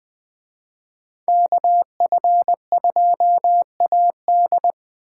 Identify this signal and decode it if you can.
Morse code, heard as a keyed tone, about 21 wpm, 700 Hz. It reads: KF2AD